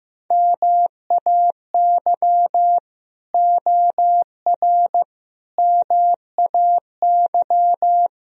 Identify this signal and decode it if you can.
Morse code, heard as a keyed tone, about 15 wpm, 700 Hz. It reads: MAY OR MAY
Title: Speed-Racing - Sets of 5 Words 15wpm